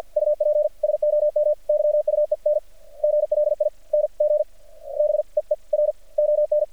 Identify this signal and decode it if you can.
Morse code, heard as a keyed tone, about 50 wpm, spaced as for 37 wpm, 600 Hz. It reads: UP SOM YUENGLING BEER ON
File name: archived / CRSnet